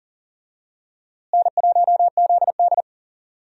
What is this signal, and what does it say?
Morse code, heard as a keyed tone, about 40 wpm, 700 Hz. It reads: N1ZD
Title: USA Call Signs - 4 Characters - Repeated 2x - 40wpm